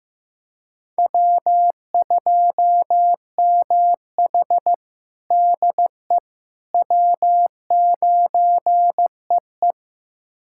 Morse code, heard as a keyed tone, about 15 wpm, 700 Hz. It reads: W2MH DE W9EE
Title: USA Callsign DE Callsign - 4 Characters - 15wpm